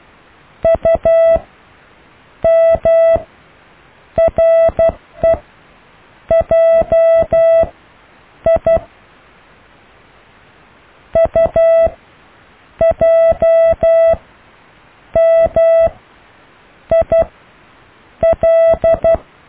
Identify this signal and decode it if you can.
Morse code, heard as a keyed tone, about 12 wpm, spaced as for 7 wpm, 660 Hz. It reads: UMREJI UJMIL
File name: cw / IJLMU